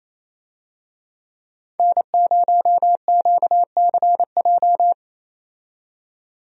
Morse code, heard as a keyed tone, about 28 wpm, 700 Hz. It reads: N0QCJ
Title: USA Call Signs - 5 Characters - 28wpm